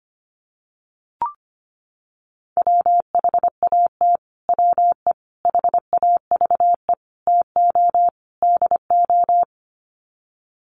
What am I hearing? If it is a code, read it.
Morse code, heard as a keyed tone, about 25 wpm, 700 Hz. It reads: WHAT WE HAVE TO DO
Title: Sets of 5 Words Encore 25wpm